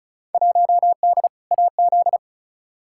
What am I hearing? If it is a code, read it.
Morse code, heard as a keyed tone, about 34 wpm, 700 Hz. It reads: 1D AZ
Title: Field Day Exchanges - 35wpm